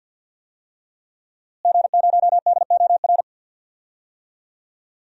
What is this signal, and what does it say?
Morse code, heard as a keyed tone, about 53 wpm, 700 Hz. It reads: G0DOR
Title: DXCC-100 Call Signs - 5 Characters - Repeated 2x - 50wpm